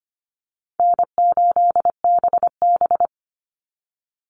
Morse code, heard as a keyed tone, about 25 wpm, 700 Hz. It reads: N8BB
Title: NA SOTA Chaser Callsigns 25wpm